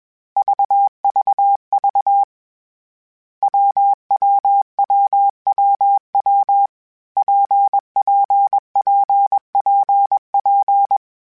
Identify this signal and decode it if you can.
Morse code, heard as a keyed tone, about 21 wpm, 800 Hz. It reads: VVV WWWWW PPPPP
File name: Variable Speed / Variable Speed-mp3s / WPJ WPJ1 sp @20